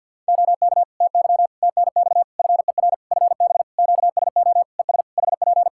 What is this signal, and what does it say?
Morse code, heard as a keyed tone, about 52 wpm, 700 Hz. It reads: OK TY TNX FER FB QSO ES HP
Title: K1USN SST QSO Practice 50wpm